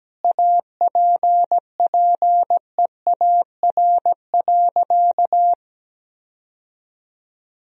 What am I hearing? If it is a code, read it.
Morse code, heard as a keyed tone, about 17 wpm, 700 Hz. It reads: APPEAR.
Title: Sentences from Top 1000 Words - Review Part 2 - 17wpm